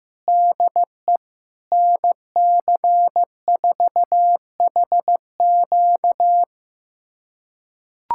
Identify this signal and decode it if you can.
Morse code, heard as a keyed tone, about 15 wpm, 700 Hz. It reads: DE NC4HQ
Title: USA Callsign DE Callsign - 5 Characters - 15wpm